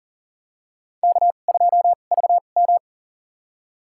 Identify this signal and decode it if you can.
Morse code, heard as a keyed tone, about 40 wpm, spaced as for 29 wpm, 700 Hz. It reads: K2VM
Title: USA Call Signs - 4 Characters - 2x Character Spacing - 40wpm